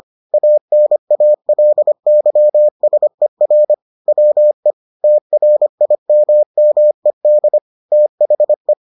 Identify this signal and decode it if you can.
Morse code, heard as a keyed tone, about 25 wpm, 600 Hz. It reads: ANALYSER WE TRIMMED THE